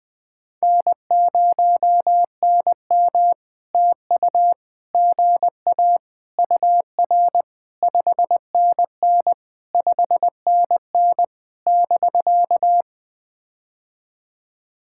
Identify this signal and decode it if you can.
Morse code, heard as a keyed tone, about 20 wpm, 700 Hz. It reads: N0NM TU GA UR 5NN 5NN <BK>
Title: POTA QSO Practice - 20wpm